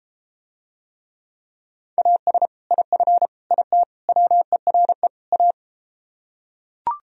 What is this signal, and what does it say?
Morse code, heard as a keyed tone, about 33 wpm, 700 Hz. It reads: AS IF IT WERE A